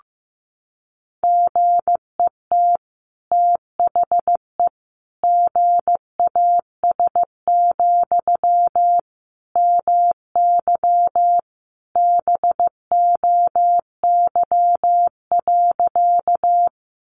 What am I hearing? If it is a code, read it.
Morse code, heard as a keyed tone, about 15 wpm, 700 Hz. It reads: GET THE GAS, MY BOY.
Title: Sentences from Top 1100 Words 15wpm